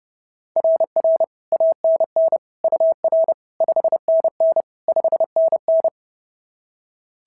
Morse code, heard as a keyed tone, about 31 wpm, 690 Hz. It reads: RR ANN UR 5NN 5NN
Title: SOTA QSO Practice 30wpm